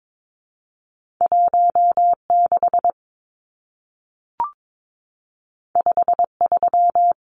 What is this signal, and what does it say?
Morse code, heard as a keyed tone, about 22 wpm, 700 Hz. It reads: 16 53
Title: Numbers - 2-Digit 22wpm